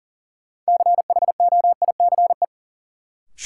Morse code, heard as a keyed tone, about 40 wpm, 700 Hz. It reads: CHOICE
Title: Sentences from Top 900 Words - New Word Drill - 40wpm